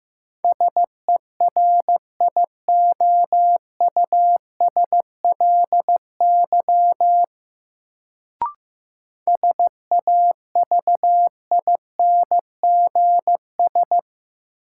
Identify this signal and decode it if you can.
Morse code, heard as a keyed tone, about 15 wpm, 700 Hz. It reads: SERIOUSLY SAVINGS
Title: Sentences from Top 1200 Words - New Word Drill - 15wpm